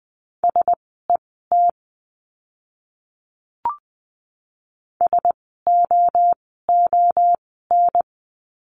Morse code, heard as a keyed tone, about 20 wpm, spaced as for 14 wpm, 700 Hz. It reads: SET SOON
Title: 054 Lesson 10f - Letter r - Words 20wpm